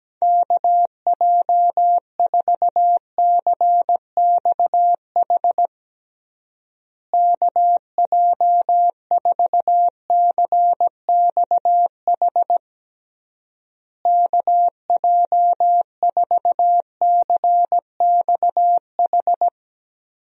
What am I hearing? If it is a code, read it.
Morse code, heard as a keyed tone, about 17 wpm, 700 Hz. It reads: KJ4CXH KJ4CXH KJ4CXH
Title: USA Call Signs - 6 Characters - Repeated 3x - 17wpm